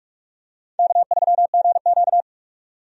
Morse code, heard as a keyed tone, about 46 wpm, 700 Hz. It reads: K3GQ
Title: USA Call Signs - 4 Characters - 45wpm